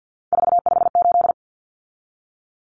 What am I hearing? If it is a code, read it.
Morse code, heard as a keyed tone, about 50 wpm, 700 Hz. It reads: 458
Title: Numbers - 3-Digit 50wpm